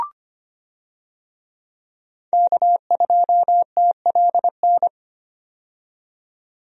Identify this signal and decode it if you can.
Morse code, heard as a keyed tone, about 25 wpm, 700 Hz. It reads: K2TLN
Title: USA Call Signs - 5 Characters - 25wpm